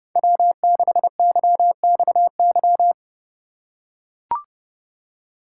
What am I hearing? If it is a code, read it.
Morse code, heard as a keyed tone, about 31 wpm, 700 Hz. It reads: W6YXY
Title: USA Call Signs - 5 Characters - 4x Character Spacing - 30wpm